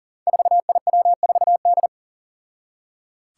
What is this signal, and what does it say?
Morse code, heard as a keyed tone, about 40 wpm, 700 Hz. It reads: 4IW4D
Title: Five Challenging Letter-Number Combinations 40wpm